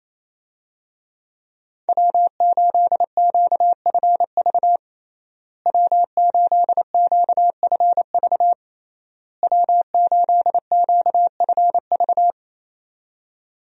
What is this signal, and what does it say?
Morse code, heard as a keyed tone, about 28 wpm, 700 Hz. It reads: W8QFV W8QFV W8QFV
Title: USA Call Signs - 5 Characters - Repeated 3x - 28wpm